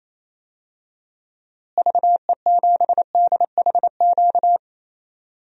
Morse code, heard as a keyed tone, about 27 wpm, 700 Hz. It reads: VE7DHQ
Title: DXCC-100 Call Signs - 6 Characters - Repeated 3x - 28wpm